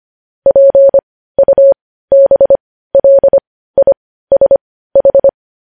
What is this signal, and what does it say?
Morse code, heard as a keyed tone, about 25 wpm, spaced as for 15 wpm, 550 Hz. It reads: PUBLISH
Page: news-headlines-in-morse-code-at-15-wpm